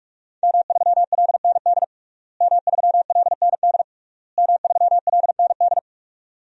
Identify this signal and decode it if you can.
Morse code, heard as a keyed tone, about 46 wpm, 700 Hz. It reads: M3LND M3LND M3LND
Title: DXCC-100 Call Signs - 5 Characters - Repeated 3x - 45wpm